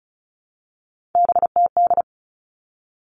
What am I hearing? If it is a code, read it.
Morse code, heard as a keyed tone, about 34 wpm, 700 Hz. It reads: BTD
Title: All Cut Numbers - 3-Digit 35wpm